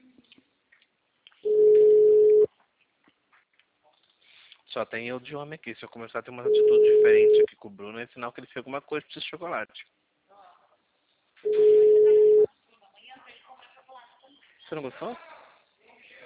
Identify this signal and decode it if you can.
Ringback tone (Europe, 425 Hz), with speech.